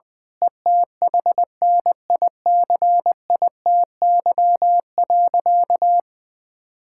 Morse code, heard as a keyed tone, about 20 wpm, 700 Hz. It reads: ETHNICITY.
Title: Sentences with 9-Word Phrases - 20wpm